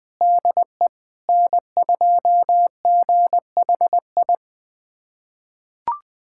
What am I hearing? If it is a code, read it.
Morse code, heard as a keyed tone, about 21 wpm, 700 Hz. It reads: DE N2GHI
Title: USA Callsign DE Callsign 20wpm